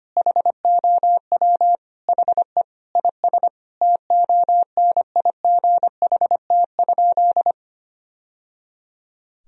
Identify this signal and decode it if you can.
Morse code, heard as a keyed tone, about 25 wpm, 700 Hz. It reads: HOW HE IS TONIGHT?
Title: Sentences from Top 1000 Words - Review Part 3 - 25wpm